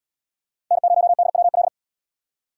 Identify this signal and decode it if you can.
Morse code, heard as a keyed tone, about 72 wpm, 700 Hz. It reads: N0DWL